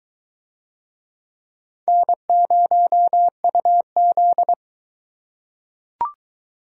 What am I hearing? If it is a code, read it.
Morse code, heard as a keyed tone, about 21 wpm, 700 Hz. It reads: N0UZ